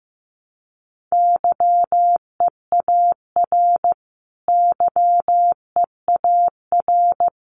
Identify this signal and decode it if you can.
Morse code, heard as a keyed tone, about 15 wpm, 700 Hz. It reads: YEAR YEAR
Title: Top 300 Words Repeated 15wpm